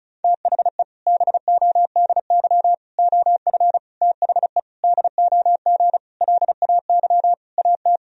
Morse code, heard as a keyed tone, about 36 wpm, 700 Hz. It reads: THE BODY OF THE DOG LAY AT
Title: Sentences from Top 1000 Words - Review Part 2 - 35wpm